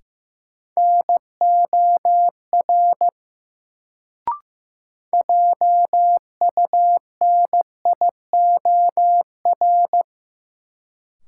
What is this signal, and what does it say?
Morse code, heard as a keyed tone, about 15 wpm, 700 Hz. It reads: NOR JUNIOR
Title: Sentences from Top 1500 Words - New Word Drill Alt - 15wpm